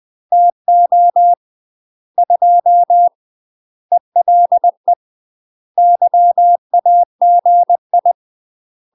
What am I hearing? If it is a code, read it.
Morse code, heard as a keyed tone, about 20 wpm, 700 Hz. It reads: TO 2 ELE YAGI
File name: UK MASTER QSOs TRAINING / 20WPM / UK MASTER QSOs 0038